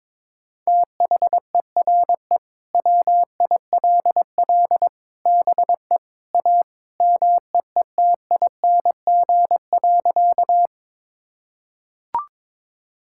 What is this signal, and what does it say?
Morse code, heard as a keyed tone, about 22 wpm, 700 Hz. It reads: THERE WILL BE A MEETING.
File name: practice / 22wpm / Sentences with 3-Word Phrases - 22wpm